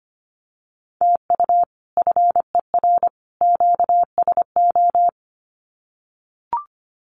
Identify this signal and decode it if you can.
Morse code, heard as a keyed tone, about 25 wpm, 700 Hz. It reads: TU FER QSO
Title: Common QSO Phrases 25wpm